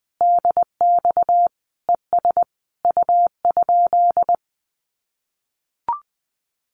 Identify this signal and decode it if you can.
Morse code, heard as a keyed tone, about 20 wpm, 700 Hz. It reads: DX ES U?
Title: Common QSO Phrases Encore 20wpm